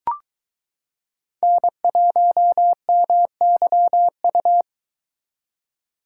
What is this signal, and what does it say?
Morse code, heard as a keyed tone, about 22 wpm, 700 Hz. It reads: N1MYU